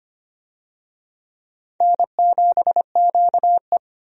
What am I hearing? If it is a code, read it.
Morse code, heard as a keyed tone, about 25 wpm, 700 Hz. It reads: N7QE
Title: USA Call Signs - 4 Characters - 25wpm